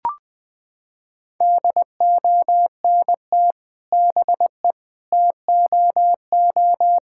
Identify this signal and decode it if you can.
Morse code, heard as a keyed tone, about 21 wpm, 700 Hz. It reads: DONT BE TOO
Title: Sentences from Top 1000 Words - Review Part 5 - 20wpm